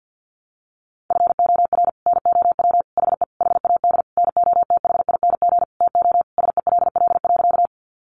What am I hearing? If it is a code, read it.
Morse code, heard as a keyed tone, about 50 wpm, 700 Hz. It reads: FOR NOW HE HAD NOTHING TO SELL.
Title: Sentences from Top 1100 Words 50wpm